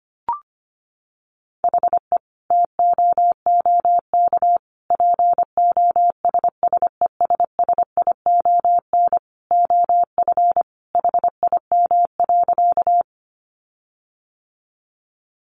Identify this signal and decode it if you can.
Morse code, heard as a keyed tone, about 25 wpm, 700 Hz. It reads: HE TOOK POSSESSION OF HIM.
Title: Sentences from Top 1400 Words 25wpm